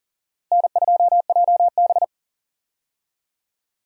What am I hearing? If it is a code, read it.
Morse code, heard as a keyed tone, about 40 wpm, 700 Hz. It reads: N2JB
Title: USA Call Signs - 4 Characters - Repeated 2x - 40wpm